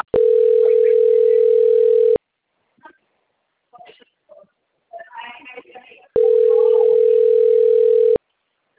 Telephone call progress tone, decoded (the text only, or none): ringback tone